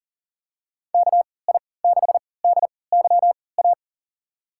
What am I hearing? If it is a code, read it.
Morse code, heard as a keyed tone, about 40 wpm, spaced as for 23 wpm, 700 Hz. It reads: KI6DYA
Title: USA Call Signs - 6 Characters - 3x Character Spacing - 40wpm